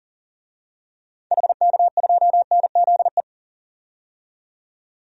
Morse code, heard as a keyed tone, about 40 wpm, 700 Hz. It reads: HK2NZE
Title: DXCC-100 Call Signs - 6 Characters - 40wpm